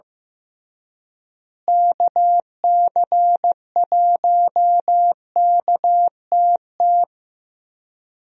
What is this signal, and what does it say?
Morse code, heard as a keyed tone, about 15 wpm, 700 Hz. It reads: KC1KTT
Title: USA Call Signs - 6 Characters - Repeated 3x - 15wpm